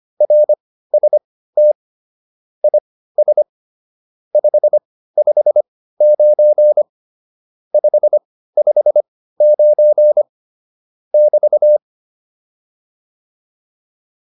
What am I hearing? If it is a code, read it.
Morse code, heard as a keyed tone, about 25 wpm, spaced as for 15 wpm, 600 Hz. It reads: RST IS 559 559 <BT>